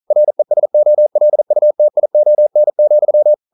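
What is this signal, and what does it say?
Morse code, heard as a keyed tone, about 42 wpm, 600 Hz. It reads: RESOLUTION,
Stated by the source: Morse code at 40 wpm, practice text, CW med Gnister.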